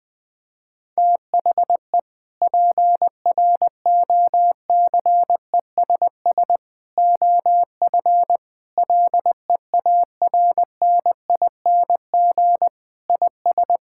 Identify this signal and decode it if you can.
Morse code, heard as a keyed tone, about 20 wpm, 700 Hz. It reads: THE PROCESS OF LEARNING IS
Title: Sentences with 3-Word Phrases - 20wpm